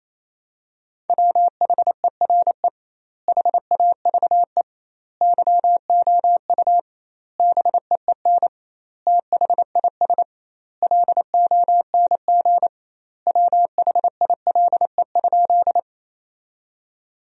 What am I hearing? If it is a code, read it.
Morse code, heard as a keyed tone, about 27 wpm, 700 Hz. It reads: WHERE HAVE YOU BEEN THIS LONG WHILE?
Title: 2x Word-Spacing - Sentences from Top 300 Words - 28wpm